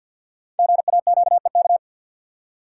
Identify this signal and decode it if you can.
Morse code, heard as a keyed tone, about 50 wpm, 700 Hz. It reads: GAQEK
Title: Code Groups - 5 Letters - 50wpm